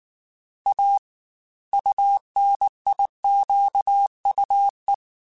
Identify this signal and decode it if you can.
Morse code, heard as a keyed tone, about 19 wpm, 800 Hz. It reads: A UNIQUE
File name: Technical Subjects / Technical Subjects-mp3s / Gates Intro @18